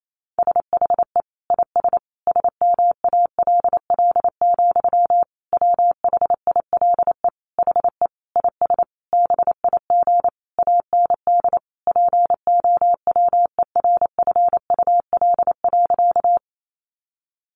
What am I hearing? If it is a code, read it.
Morse code, heard as a keyed tone, about 28 wpm, 700 Hz. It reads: SHE IS SMALL, WHILE HE IS BIG AND POWERFUL.
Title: Sentences from Top 1300 Words 28wpm